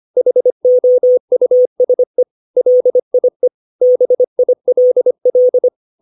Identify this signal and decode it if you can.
Morse code, heard as a keyed tone, about 25 wpm, 500 Hz. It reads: HOUSE LIE BILL
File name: 632 words / 632 Words 25WPM